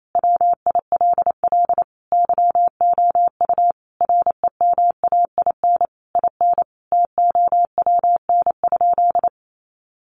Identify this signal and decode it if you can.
Morse code, heard as a keyed tone, about 28 wpm, 700 Hz. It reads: WILL YOU REMAIN IN TOWN?
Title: Sentences from Top 1200 Words 28wpm